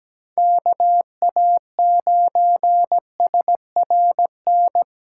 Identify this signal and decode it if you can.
Morse code, heard as a keyed tone, about 17 wpm, 700 Hz. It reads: KA9SRN
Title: USA Call Signs - 6 Characters - 3x Character Spacing - 17wpm